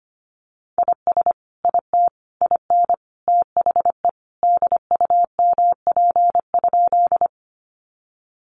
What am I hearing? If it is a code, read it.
Morse code, heard as a keyed tone, about 25 wpm, 700 Hz. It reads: IS IT IN THE DUMP?
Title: Sentences from Top 1300 Words 25wpm